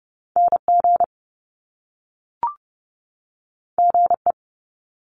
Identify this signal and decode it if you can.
Morse code, heard as a keyed tone, about 30 wpm, 700 Hz. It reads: NG GE
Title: Top Two Letter Combinations in Words 30wpm